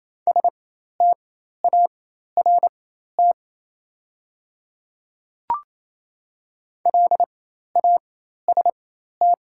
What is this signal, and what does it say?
Morse code, heard as a keyed tone, about 28 wpm, spaced as for 13 wpm, 700 Hz. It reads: START LAST
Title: 4x Character Spacing - Top 300 Words - 28wpm